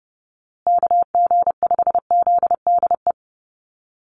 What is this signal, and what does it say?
Morse code, heard as a keyed tone, about 30 wpm, 700 Hz. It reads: KG5ZDE
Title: USA Call Signs Repeated 3x 30wpm